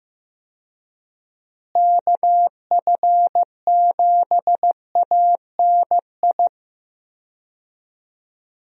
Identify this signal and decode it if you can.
Morse code, heard as a keyed tone, about 15 wpm, 700 Hz. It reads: KF7ANI
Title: USA Call Signs - 6 Characters - Repeated 2x - 15wpm